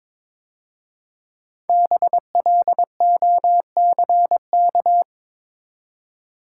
Morse code, heard as a keyed tone, about 22 wpm, 690 Hz. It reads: BLOCK
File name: 5 Letter Words - First 3 Letters Given - 22wpm